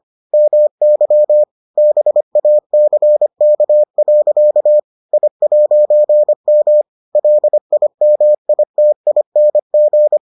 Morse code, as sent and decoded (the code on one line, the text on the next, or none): -- -.-- / -... .- -.-. -.- .-.-.- / .. .----. -- / .-.. .. -- .. - .. -. --.
MY BACK. I'M LIMITING